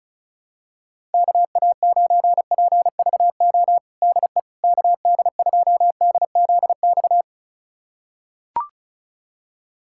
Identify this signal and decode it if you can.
Morse code, heard as a keyed tone, about 36 wpm, 700 Hz. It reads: KA9PVO DE KD2DZX
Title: USA Callsign DE Callsign - 6 Characters - 35wpm